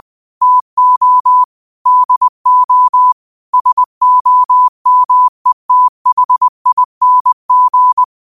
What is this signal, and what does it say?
Morse code, heard as a keyed tone, about 20 wpm, 1000 Hz. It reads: TO DO SOMETHING
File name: qotdfiles / 20wpm